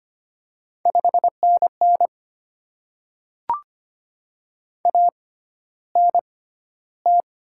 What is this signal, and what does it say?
Morse code, heard as a keyed tone, about 25 wpm, spaced as for 13 wpm, 700 Hz. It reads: 5NN ANT
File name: Top 20 QSO Elements - 6x Character Spacing - 25wpm